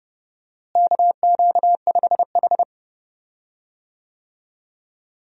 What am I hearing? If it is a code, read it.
Morse code, heard as a keyed tone, about 32 wpm, 700 Hz. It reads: KQ5H
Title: USA Call Signs - 4 Characters - Repeated 3x - 30wpm